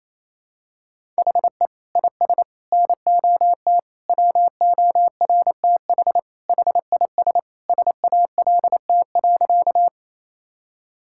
Morse code, heard as a keyed tone, about 28 wpm, 700 Hz. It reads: HE IS NOT WORTH HIS SALT.
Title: Sentences from Top 1000 Words - Review Part 5 - 28wpm